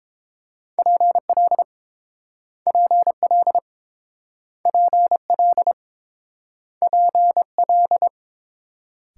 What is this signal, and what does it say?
Morse code, heard as a keyed tone, about 29 wpm, 700 Hz. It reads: PL PL PL PL